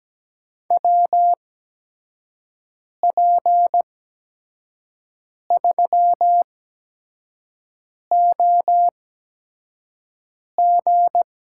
Morse code, heard as a keyed tone, about 17 wpm, spaced as for 5 wpm, 700 Hz. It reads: WP3OG
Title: USA Call Signs - 5 Characters - 8x Character Spacing - 17wpm